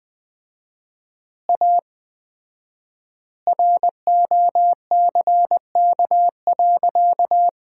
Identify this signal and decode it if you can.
Morse code, heard as a keyed tone, about 20 wpm, 700 Hz. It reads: A ROCK.
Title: 4x Word-Spacing - Sentences from Top 400 Words - 20wpm